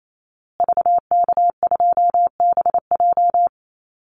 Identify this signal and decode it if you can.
Morse code, heard as a keyed tone, about 28 wpm, 700 Hz. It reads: VK2BJ
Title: Call Signs Repeated 2x - 28wpm